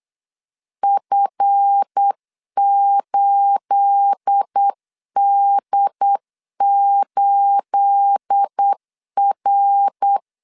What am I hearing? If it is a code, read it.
Morse code, heard as a keyed tone, about 8 wpm, 800 Hz. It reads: F8D8R
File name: LEZIONI / 9 28 8wpm 3512